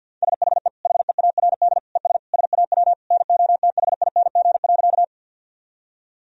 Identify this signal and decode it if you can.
Morse code, heard as a keyed tone, about 50 wpm, 700 Hz. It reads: SHE HEARD ES SAW NOTHING.
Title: Sentences from Top 500 Words and Abbreviations 50wpm